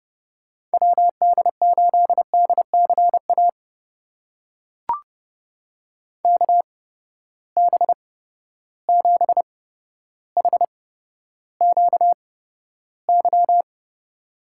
Morse code, heard as a keyed tone, about 30 wpm, spaced as for 13 wpm, 700 Hz. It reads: WD8DCA KB7HQY
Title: USA Call Signs - 6 Characters - 8x Character Spacing - 30wpm